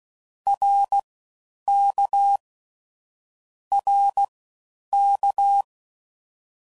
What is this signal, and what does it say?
Morse code, heard as a keyed tone, about 16 wpm, spaced as for 9 wpm, 800 Hz. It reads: RK RK